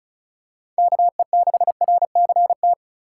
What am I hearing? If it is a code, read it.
Morse code, heard as a keyed tone, about 36 wpm, 700 Hz. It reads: KE6RCT
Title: USA Call Signs - 3x Character Spacing - 35wpm